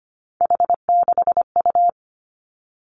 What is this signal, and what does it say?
Morse code, heard as a keyed tone, about 25 wpm, 700 Hz. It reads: H6U
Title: Three Challenging Letter-Number Combinations 25wpm